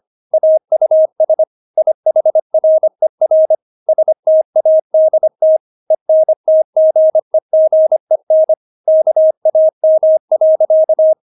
Morse code, heard as a keyed tone, about 25 wpm, 630 Hz. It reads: AUS IHRER STADT ENTGEGEN KAM.
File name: Witz6 25wpm